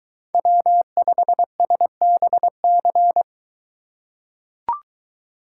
Morse code, heard as a keyed tone, about 22 wpm, 700 Hz. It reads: W5SBC